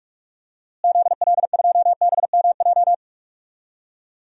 Morse code, heard as a keyed tone, about 43 wpm, 700 Hz. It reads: ZL2BMJ
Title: DXCC-100 Call Signs - 6 Characters - Repeated 3x - 45wpm